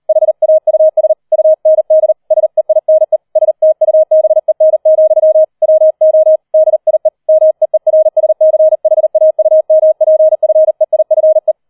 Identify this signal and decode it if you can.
Morse code, heard as a keyed tone, about 38 wpm, 630 Hz. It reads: HAUS UND SEINE STUBEN, WO DIE MEERSCHAUMPFEIFE
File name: 004 Vierter Streich 40wpm